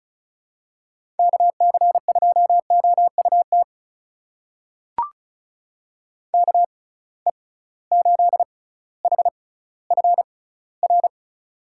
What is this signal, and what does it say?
Morse code, heard as a keyed tone, about 36 wpm, spaced as for 18 wpm, 690 Hz. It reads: KC2OUT KE8HFR
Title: USA Call Signs - 6 Characters - 6x Character Spacing - 35wpm